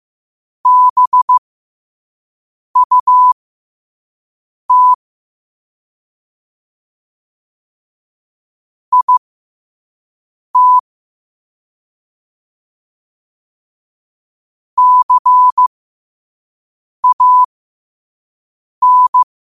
Morse code, heard as a keyed tone, about 15 wpm, spaced as for 5 wpm, 1000 Hz. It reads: BUT IT CAN
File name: qotdfiles / 5wpm